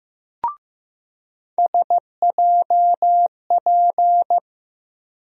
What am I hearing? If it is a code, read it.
Morse code, heard as a keyed tone, about 15 wpm, 700 Hz. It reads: SJP
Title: USA Call Sign Postfixes - 3 Letters 15wpm